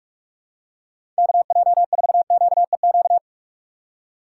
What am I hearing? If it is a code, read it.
Morse code, heard as a keyed tone, about 44 wpm, 700 Hz. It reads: KJ4QEQ